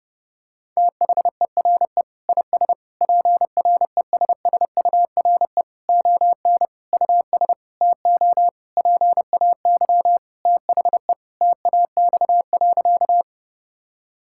Morse code, heard as a keyed tone, about 32 wpm, 700 Hz. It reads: THERE IS PRESSURE ON US TO PAY THE TAX.
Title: Sentences from Top 1000 Words - Review Part 6 - 30wpm